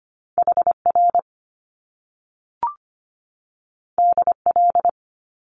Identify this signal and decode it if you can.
Morse code, heard as a keyed tone, about 25 wpm, 700 Hz. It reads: HR DL
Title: Top Two Letter Combinations in Words 25wpm